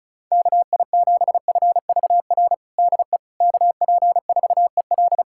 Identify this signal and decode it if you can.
Morse code, heard as a keyed tone, about 36 wpm, 700 Hz. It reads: KI7FVR DE KP4EL
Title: USA Callsign DE Callsign 35wpm